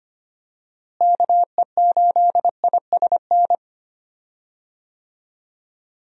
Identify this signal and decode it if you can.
Morse code, heard as a keyed tone, about 25 wpm, 700 Hz. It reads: KE8ISN
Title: USA Call Signs - 6 Characters - Repeated 2x - 25wpm